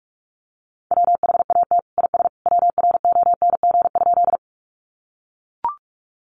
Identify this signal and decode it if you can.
Morse code, heard as a keyed tone, about 44 wpm, 700 Hz. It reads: WHAT IS WRONG?
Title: Sentences from Top 900 Words 45wpm